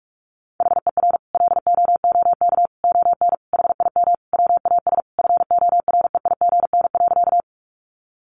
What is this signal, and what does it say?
Morse code, heard as a keyed tone, about 44 wpm, 700 Hz. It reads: HER LOOK ON HIM WAS FOREIGN.
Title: Sentences from Top 1400 Words 45wpm